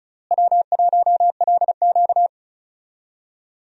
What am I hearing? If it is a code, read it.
Morse code, heard as a keyed tone, about 36 wpm, 700 Hz. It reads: W1LQ
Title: USA Call Signs - 4 Characters - Repeated 3x - 35wpm